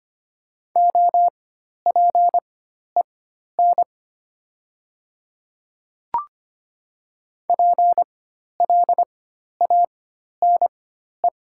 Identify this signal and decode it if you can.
Morse code, heard as a keyed tone, about 25 wpm, spaced as for 12 wpm, 700 Hz. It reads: OPEN PLANE
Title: 4x Character Spacing - Top 500 Words - 25wpm